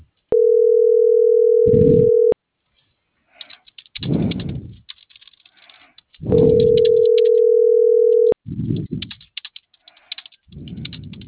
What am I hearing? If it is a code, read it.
Ringback tone (North America), 440 + 480 Hz.